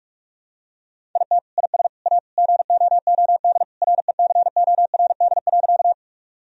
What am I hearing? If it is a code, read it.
Morse code, heard as a keyed tone, about 46 wpm, 700 Hz. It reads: IT IS A GOOD RECORD.
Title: Sentences from Top 1000 Words - Review Part 3 - 45wpm